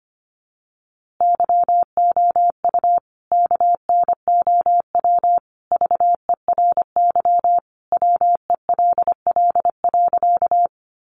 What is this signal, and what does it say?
Morse code, heard as a keyed tone, about 25 wpm, 700 Hz. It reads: YOU KNOW VERY WELL.
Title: A Sound Of Thunder 25wpm